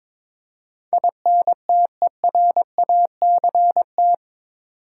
Morse code, heard as a keyed tone, about 22 wpm, 700 Hz. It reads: INTERACT
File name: practice / 22wpm / Eight Letter Words - 22wpm